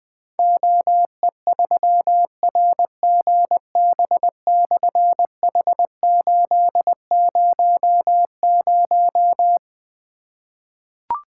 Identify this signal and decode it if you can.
Morse code, heard as a keyed tone, about 20 wpm, 700 Hz. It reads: OE3RGB/H800
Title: Difficult Call Signs 20wpm